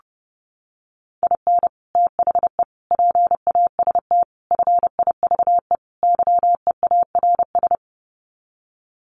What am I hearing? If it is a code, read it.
Morse code, heard as a keyed tone, about 28 wpm, 700 Hz. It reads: IN THE PAST FIVE YEARS
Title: Sets of 5 Words Encore 30wpm